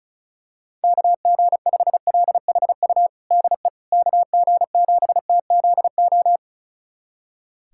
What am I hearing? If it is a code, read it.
Morse code, heard as a keyed tone, about 36 wpm, 700 Hz. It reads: KG5LHU DE KG7TZO
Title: USA Callsign DE Callsign - 6 Characters - 35wpm